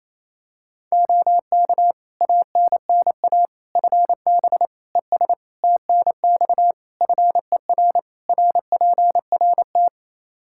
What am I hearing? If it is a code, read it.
Morse code, heard as a keyed tone, about 28 wpm, 700 Hz. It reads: OK ANNA FB ES TNX FER RPRT
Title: Practice QSO - Exchange 2 - Info - 28wpm